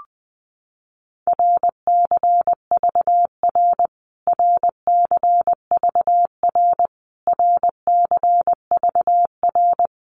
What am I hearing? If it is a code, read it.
Morse code, heard as a keyed tone, about 20 wpm, 700 Hz. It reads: RCVR RCVR RCVR
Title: Abbreviations Repeated 3x - 20wpm